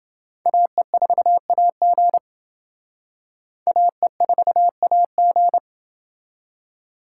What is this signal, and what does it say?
Morse code, heard as a keyed tone, about 28 wpm, 700 Hz. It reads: AE4AG AE4AG